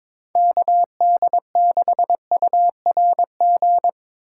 Morse code, heard as a keyed tone, about 22 wpm, 700 Hz. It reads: KD6URG
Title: USA Callsign DE Callsign 22wpm